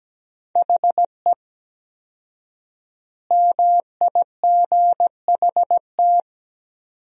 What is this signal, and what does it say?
Morse code, heard as a keyed tone, about 17 wpm, 700 Hz. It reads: HE MIGHT
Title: 4x Word-Spacing - Sentences from Top 300 Words - 17wpm